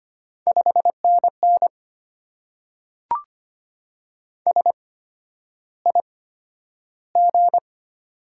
Morse code, heard as a keyed tone, about 25 wpm, spaced as for 11 wpm, 700 Hz. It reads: 5NN SIG
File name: Top 90 QSO Elements - 8x Character Spacing - 25wpm